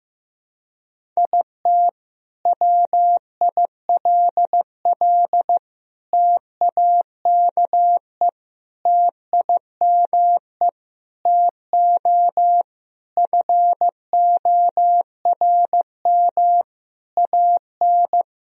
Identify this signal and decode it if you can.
Morse code, heard as a keyed tone, about 15 wpm, 700 Hz. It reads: IT WILL TAKE TIME TO FORM AN
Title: Sentences from Top 1000 Words - Review Part 1 - 15wpm